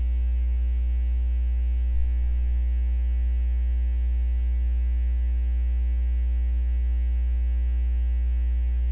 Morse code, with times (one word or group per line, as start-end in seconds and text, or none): none